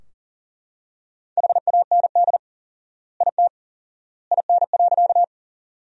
Morse code, heard as a keyed tone, about 37 wpm, 700 Hz. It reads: HAND IT IN.
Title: 4x Word-Spacing - Sentences from Top 200 Words - 40wpm